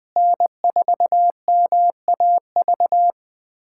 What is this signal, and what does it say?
Morse code, heard as a keyed tone, about 20 wpm, 700 Hz. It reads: N4MAV
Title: USA Callsign DE Callsign - 5 Characters - 20wpm